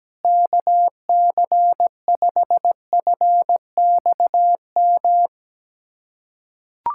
Morse code, heard as a keyed tone, about 17 wpm, 700 Hz. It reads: KC5FXM
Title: USA Callsign DE Callsign 17wpm